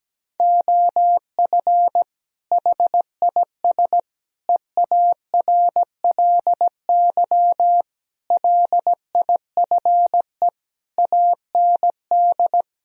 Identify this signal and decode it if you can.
Morse code, heard as a keyed tone, about 17 wpm, 700 Hz. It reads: OF HIS EARLY LIFE AND
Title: Sentences from Top 1000 Words - Review Part 4 - 17wpm